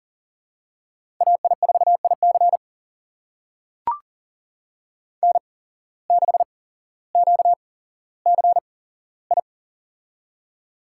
Morse code, heard as a keyed tone, about 40 wpm, spaced as for 17 wpm, 700 Hz. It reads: AI4IC N6QCI
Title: USA Call Signs - 5 Characters - 8x Character Spacing - 40wpm